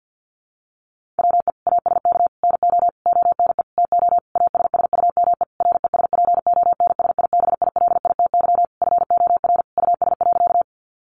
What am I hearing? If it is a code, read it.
Morse code, heard as a keyed tone, about 50 wpm, 700 Hz. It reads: WE ASK NO ONE TO ASSUME RESPONSIBILITY FOR US.
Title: Sentences from Top 1500 Words 50wpm